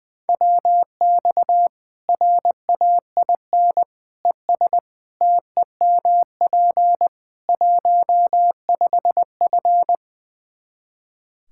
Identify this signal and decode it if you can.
Morse code, heard as a keyed tone, about 20 wpm, 700 Hz. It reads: WX RAIN ES TEMP 15F
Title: Practice QSO - Exchange 2 - Info - 20wpm